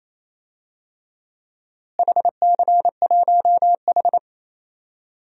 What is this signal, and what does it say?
Morse code, heard as a keyed tone, about 27 wpm, 700 Hz. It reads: HC1H
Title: DXCC-100 Call Signs - 4 Characters - Repeated 2x - 28wpm